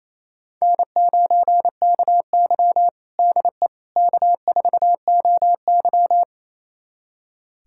Morse code, heard as a keyed tone, about 28 wpm, 700 Hz. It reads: N9KY DE K4OY
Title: USA Callsign DE Callsign - 4 Characters - 28wpm